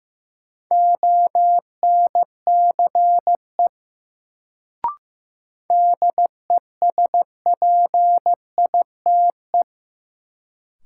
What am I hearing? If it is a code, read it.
Morse code, heard as a keyed tone, about 15 wpm, 700 Hz. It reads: ONCE DESPITE
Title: Sets of 4 Words - New Word Drill - 15wpm